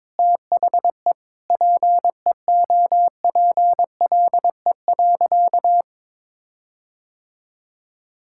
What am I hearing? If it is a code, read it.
Morse code, heard as a keyed tone, about 21 wpm, 700 Hz. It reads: THE PEOPLE.
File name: practice / 22wpm / Sentences with 8-Word Phrases - 22wpm